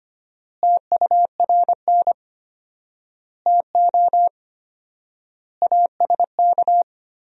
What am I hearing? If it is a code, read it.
Morse code, heard as a keyed tone, about 25 wpm, 700 Hz. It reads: TURN TO ASK
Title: 4x Word-Spacing - Sentences from Top 200 Words - 25wpm